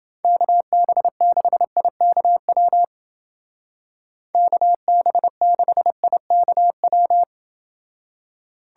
Morse code, heard as a keyed tone, about 28 wpm, 700 Hz. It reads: KB6IKW KB6IKW